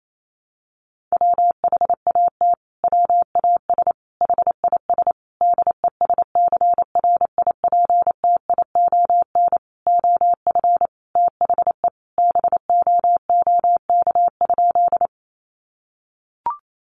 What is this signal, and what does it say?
Morse code, heard as a keyed tone, about 28 wpm, 700 Hz. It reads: WHAT WAS HIS DESCRIPTION OF THE BOOK?
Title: Sentences from Top 1300 Words 28wpm